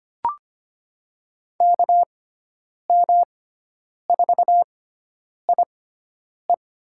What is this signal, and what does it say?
Morse code, heard as a keyed tone, about 25 wpm, spaced as for 9 wpm, 700 Hz. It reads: KM4IE
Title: USA Call Signs - 6 Characters - 6x Character Spacing - 25wpm